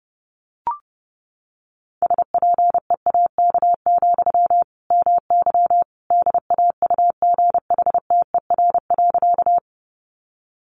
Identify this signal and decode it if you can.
Morse code, heard as a keyed tone, about 30 wpm, 700 Hz. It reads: SPEAK, MY DAUGHTER.
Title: Sentences from Top 1500 Words 30wpm